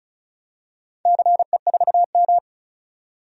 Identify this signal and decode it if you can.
Morse code, heard as a keyed tone, about 34 wpm, 700 Hz. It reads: CE4M
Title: DXCC-100 Call Signs - 4 Characters - 35wpm